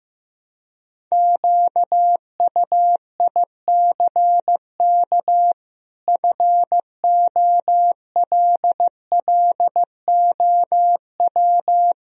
Morse code, heard as a keyed tone, about 15 wpm, 700 Hz. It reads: QUICK FOLLOW
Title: Sentences from Top 1000 Words - Review Part 1 - 15wpm